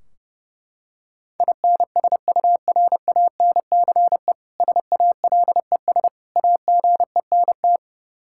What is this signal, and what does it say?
Morse code, heard as a keyed tone, about 30 wpm, 700 Hz. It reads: INSURANCE SALES AGENT
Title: K1USN SST QSO Practice 30wpm